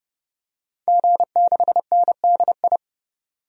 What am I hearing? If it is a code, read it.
Morse code, heard as a keyed tone, about 30 wpm, 700 Hz. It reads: G6NDI
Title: DXCC-100 Call Signs - 5 Characters - 30wpm